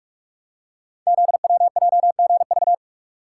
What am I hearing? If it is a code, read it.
Morse code, heard as a keyed tone, about 44 wpm, 690 Hz. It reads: ZWJGV
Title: Code Groups - 5 Letters - 45wpm